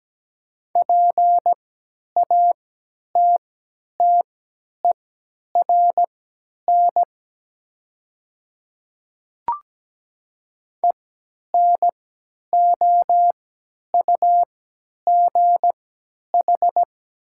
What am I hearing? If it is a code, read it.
Morse code, heard as a keyed tone, about 17 wpm, spaced as for 10 wpm, 700 Hz. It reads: PATTERN ENOUGH
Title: 3x Character Spacing - Top 500 Words - 17wpm